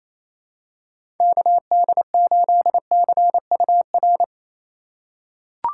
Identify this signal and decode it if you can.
Morse code, heard as a keyed tone, about 28 wpm, 700 Hz. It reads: KD8CUR
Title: USA Call Signs - 3x Character Spacing - 28wpm